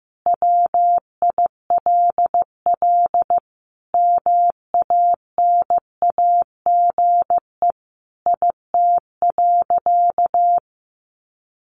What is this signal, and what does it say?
Morse code, heard as a keyed tone, about 15 wpm, 700 Hz. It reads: WILL MANAGE IT.
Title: Sentences from Top 1400 Words 15wpm